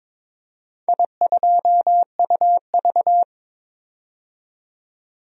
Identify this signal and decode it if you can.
Morse code, heard as a keyed tone, about 22 wpm, 700 Hz. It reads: I2UV
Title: DXCC-100 Call Signs - 4 Characters - 22wpm